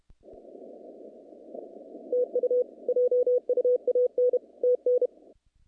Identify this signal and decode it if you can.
Morse code, heard as a keyed tone, about 31 wpm, 500 Hz. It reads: TU JUAN TN
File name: contests / naqp-cw-k5tr-op-k5pi- -2300